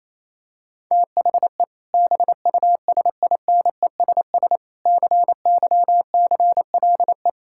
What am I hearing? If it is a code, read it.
Morse code, heard as a keyed tone, about 27 wpm, 700 Hz. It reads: THE BUSINESS CYCLE
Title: Sentences from Top 1000 Words - Review Part 6 - 28wpm